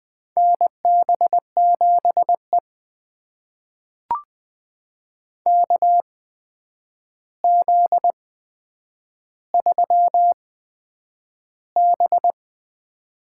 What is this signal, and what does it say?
Morse code, heard as a keyed tone, about 20 wpm, spaced as for 9 wpm, 700 Hz. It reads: NB7E KZ3B
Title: USA Call Signs - 4 Characters - 8x Character Spacing - 20wpm